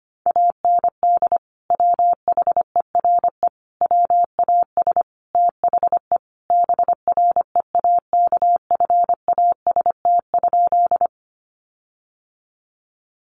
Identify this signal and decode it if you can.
Morse code, heard as a keyed tone, about 25 wpm, 700 Hz. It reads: AND WHERE WAS THE BREAKFAST?
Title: Sentences from Top 1500 Words 25wpm